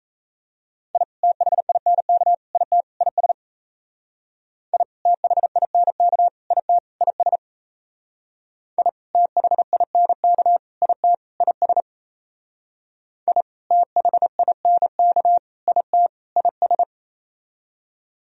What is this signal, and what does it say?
Morse code, heard as a keyed tone, about 34 wpm, 700 Hz. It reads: I THINK IT IS I THINK IT IS I THINK IT IS I THINK IT IS
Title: Speed-Racing - Sets of 4 Words 28wpm